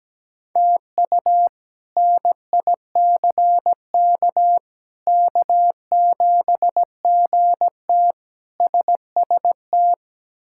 Morse code, heard as a keyed tone, about 17 wpm, 700 Hz. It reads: TU NICK K7GT SST
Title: K1USN SST QSO Practice - 17wpm